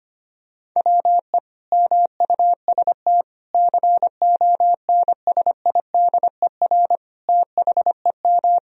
Morse code, heard as a keyed tone, about 25 wpm, 700 Hz. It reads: WE MUST CONSIDER THEM
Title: Sentences from Top 1000 Words - Review Part 4 - 25wpm